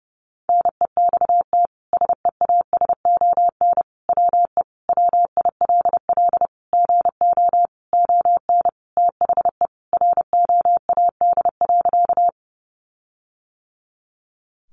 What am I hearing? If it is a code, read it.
Morse code, heard as a keyed tone, about 30 wpm, 700 Hz. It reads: NEXT SEASON WE WILL GO ON THE ROAD.
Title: Sentences from Top 1400 Words 30wpm